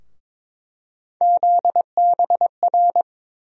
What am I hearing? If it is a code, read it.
Morse code, heard as a keyed tone, about 22 wpm, 700 Hz. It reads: ZBR
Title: USA Call Sign Postfixes - 3 Letters 22wpm